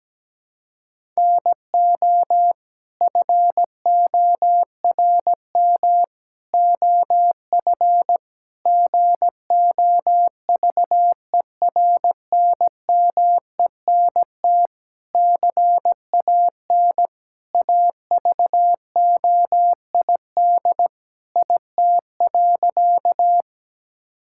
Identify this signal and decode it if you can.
Morse code, heard as a keyed tone, about 17 wpm, 700 Hz. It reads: NO FORM OF GOVERNMENT CAN AVOID IT.
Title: Sentences from Top 1000 Words - Review Part 6 - 17wpm